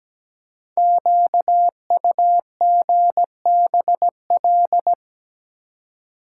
Morse code, heard as a keyed tone, about 17 wpm, 700 Hz. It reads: QUGBL
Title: Code Groups - 5 Letters - 17wpm